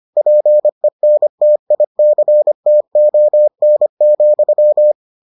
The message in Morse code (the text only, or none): PENTICTON,